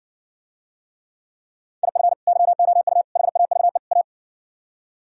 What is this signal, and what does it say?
Morse code, heard as a keyed tone, about 59 wpm, 700 Hz. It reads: IF YOU HAVE A